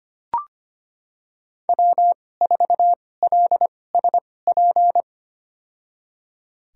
Morse code, heard as a keyed tone, about 25 wpm, spaced as for 18 wpm, 700 Hz. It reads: W4LSP
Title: USA Call Signs - 5 Characters - 2x Character Spacing - 25wpm